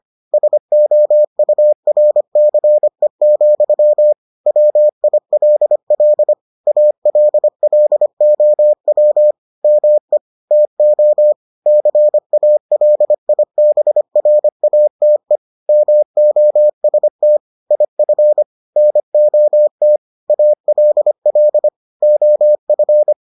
Morse code, heard as a keyed tone, about 25 wpm, 600 Hz. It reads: SOURCE, WILL ALLOW ME TO CALIBRATE MOST IF NOT ALL OF